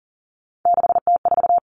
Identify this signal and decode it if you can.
Morse code, heard as a keyed tone, about 40 wpm, 700 Hz. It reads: 6T4
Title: Three Challenging Letter-Number Combinations 40wpm